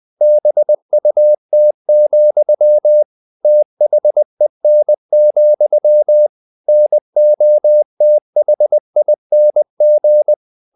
Morse code, heard as a keyed tone, about 20 wpm, 600 Hz. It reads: BUT, THEN, NOTHING
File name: the old man and the sea part 2 20wpm